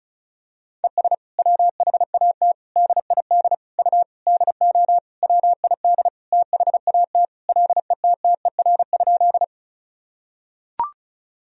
Morse code, heard as a keyed tone, about 34 wpm, 700 Hz. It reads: ES WHAT DID U DO WID THAT LETTER?
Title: Sentences from Top 300 Words and Abbreviations 35wpm